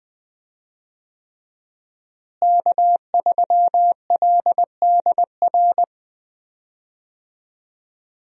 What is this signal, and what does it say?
Morse code, heard as a keyed tone, about 20 wpm, 700 Hz. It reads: K3LDR
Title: USA Call Signs - 5 Characters - Repeated 3x - 20wpm